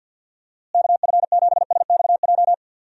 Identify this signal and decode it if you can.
Morse code, heard as a keyed tone, about 51 wpm, 700 Hz. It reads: KF7SXJ
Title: USA Call Signs - 6 Characters - Repeated 2x - 50wpm